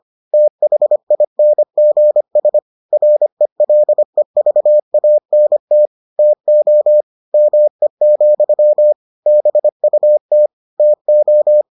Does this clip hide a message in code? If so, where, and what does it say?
Yes: Morse code from 0.3 to 11.6 s, heard as a keyed tone, about 25 wpm, 600 Hz. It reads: THINGS RELEVANT TO ME, BUT TO